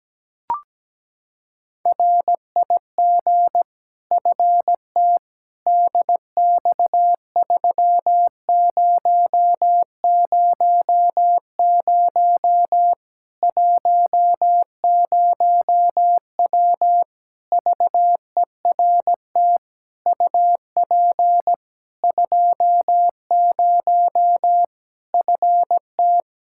Morse code, heard as a keyed tone, about 17 wpm, 700 Hz. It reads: RIG FT DX3000 10W VERT UP 20 FT
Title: Practice QSO - Exchange 2 - Info - 17wpm